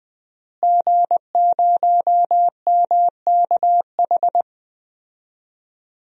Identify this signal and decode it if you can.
Morse code, heard as a keyed tone, about 21 wpm, 700 Hz. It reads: G0MKH
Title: DXCC-100 Call Signs - 5 Characters - Repeated 2x - 20wpm